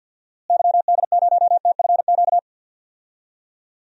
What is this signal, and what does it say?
Morse code, heard as a keyed tone, about 48 wpm, 700 Hz. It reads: YD0TFQ
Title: DXCC-100 Call Signs - 6 Characters - Repeated 3x - 50wpm